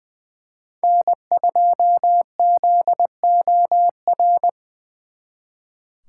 Morse code, heard as a keyed tone, about 24 wpm, 700 Hz. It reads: N2ZOR
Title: Speed Racing - USA Call Signs 20wpm